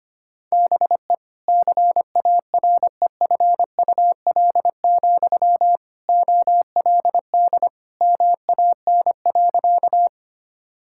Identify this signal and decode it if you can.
Morse code, heard as a keyed tone, about 25 wpm, 700 Hz. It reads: BE CAREFUL, OLD MAN.
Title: Sentences from Top 1600 Words 25wpm